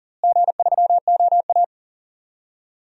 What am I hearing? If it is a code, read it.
Morse code, heard as a keyed tone, about 40 wpm, 700 Hz. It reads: G3OA
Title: DXCC-100 Call Signs - 4 Characters - Repeated 3x - 40wpm